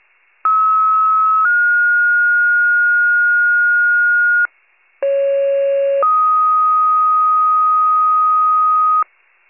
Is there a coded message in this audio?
Yes: two-tone page from 0.4 to 9.0 s, heard as a one-second tone, then a three-second tone, about 1290 then 1430 Hz, about 550 then 1250 Hz.